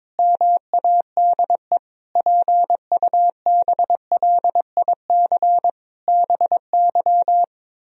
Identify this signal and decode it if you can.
Morse code, heard as a keyed tone, about 22 wpm, 700 Hz. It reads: MADE PUBLIC BY
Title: Sentences from Top 1000 Words - Review Part 3 - 22wpm